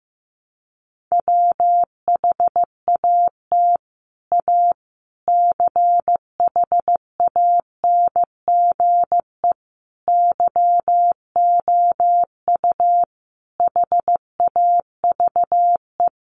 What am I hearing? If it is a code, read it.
Morse code, heard as a keyed tone, about 15 wpm, 700 Hz. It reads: WHAT A CHANGE YOU HAVE
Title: Sentences from Top 600 Words 15wpm